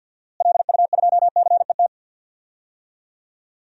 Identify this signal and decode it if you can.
Morse code, heard as a keyed tone, about 47 wpm, 700 Hz. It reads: LU2CET
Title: DXCC-100 Call Signs - 6 Characters - Repeated 2x - 50wpm